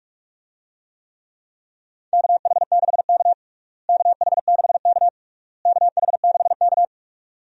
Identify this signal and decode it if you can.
Morse code, heard as a keyed tone, about 46 wpm, 700 Hz. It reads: KH6K KH6K KH6K
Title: DXCC-100 Call Signs - 4 Characters - Repeated 3x - 45wpm